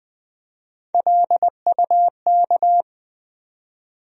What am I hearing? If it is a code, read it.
Morse code, heard as a keyed tone, about 20 wpm, 700 Hz. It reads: LUK
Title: USA Call Sign Postfixes - 3 Letters 20wpm